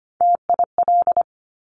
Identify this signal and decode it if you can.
Morse code, heard as a keyed tone, about 25 wpm, 700 Hz. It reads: TIL
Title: Top Three Letter Combinations in Words 25wpm